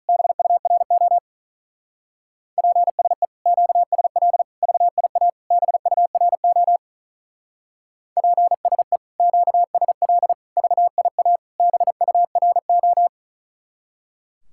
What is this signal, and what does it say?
Morse code, heard as a keyed tone, about 46 wpm, 700 Hz. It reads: BURO PSE QSL VIA BURO PSE QSL VIA BURO